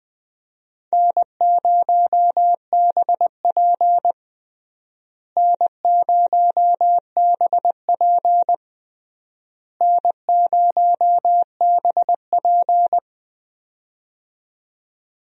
Morse code, heard as a keyed tone, about 20 wpm, 700 Hz. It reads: N0BP N0BP N0BP
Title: USA Call Signs - 4 Characters - Repeated 3x - 20wpm